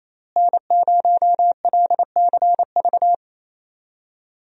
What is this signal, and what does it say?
Morse code, heard as a keyed tone, about 28 wpm, 700 Hz. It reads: N0LCV
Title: USA Call Signs - 5 Characters - Repeated 3x - 28wpm